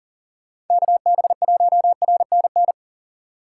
Morse code, heard as a keyed tone, about 53 wpm, 700 Hz. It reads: KB1RNN